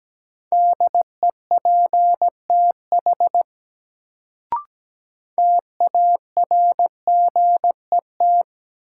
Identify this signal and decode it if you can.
Morse code, heard as a keyed tone, about 17 wpm, 700 Hz. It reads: DEPTH TARGET
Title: Sentences from Top 1200 Words - New Word Drill - 17wpm